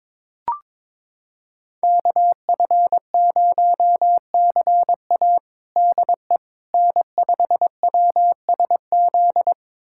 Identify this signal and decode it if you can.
Morse code, heard as a keyed tone, about 22 wpm, 700 Hz. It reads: KF0CA DE N5WSZ
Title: USA Callsign DE Callsign - 5 Characters - 22wpm